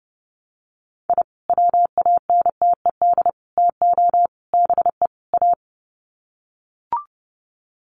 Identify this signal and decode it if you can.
Morse code, heard as a keyed tone, about 30 wpm, 700 Hz. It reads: I WANTED TO BE A
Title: Sets of 5 Words Encore2 30wpm